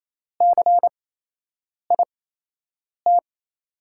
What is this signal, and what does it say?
Morse code, heard as a keyed tone, about 28 wpm, spaced as for 8 wpm, 700 Hz. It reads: CIT
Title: USA Call Signs - 5 Characters - 8x Character Spacing - 28wpm